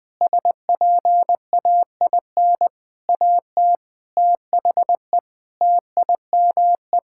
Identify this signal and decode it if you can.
Morse code, heard as a keyed tone, about 20 wpm, 700 Hz. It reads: SPAIN AT THE TIME
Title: Sentences with 6-Word Phrases - 20wpm